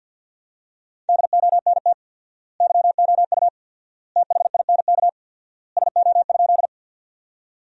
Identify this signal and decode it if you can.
Morse code, heard as a keyed tone, about 53 wpm, 700 Hz. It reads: DONT YOU THINK SO?
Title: 4x Word-Spacing - Sentences from Top 300 Words - 50wpm